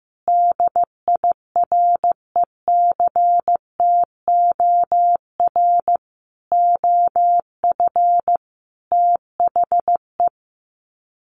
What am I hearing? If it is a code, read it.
Morse code, heard as a keyed tone, about 15 wpm, 700 Hz. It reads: DIRECTOR OF THE
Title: Sets of 5 Words Encore2 15wpm